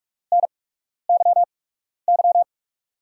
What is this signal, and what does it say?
Morse code, heard as a keyed tone, about 44 wpm, spaced as for 12 wpm, 700 Hz. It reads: NYY
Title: USA Call Signs - 8x Character Spacing - 45wpm